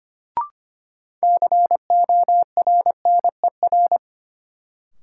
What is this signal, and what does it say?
Morse code, heard as a keyed tone, about 25 wpm, 700 Hz. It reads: CORNER
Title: Sentences from Top 1600 Words - New Word Drill - 25wpm